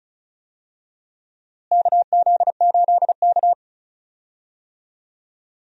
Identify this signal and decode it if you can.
Morse code, heard as a keyed tone, about 36 wpm, 700 Hz. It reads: KZ8K
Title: USA Call Signs - 4 Characters - Repeated 3x - 35wpm